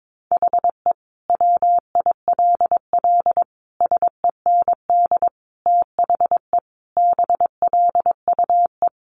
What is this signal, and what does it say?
Morse code, heard as a keyed tone, about 22 wpm, 700 Hz. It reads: HE WILL SEND THE BLUE
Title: Additional Easy Phrases 22wpm